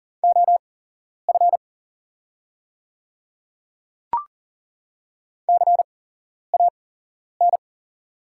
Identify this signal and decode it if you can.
Morse code, heard as a keyed tone, about 40 wpm, spaced as for 11 wpm, 690 Hz. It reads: OF CAN
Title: 8x Character Spacing - Top 100 Words - 40wpm